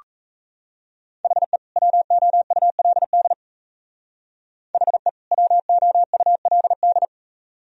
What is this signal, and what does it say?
Morse code, heard as a keyed tone, about 34 wpm, 700 Hz. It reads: HE WOULD HE WOULD
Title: Speed-Racing - Sets of 2 Words 28wpm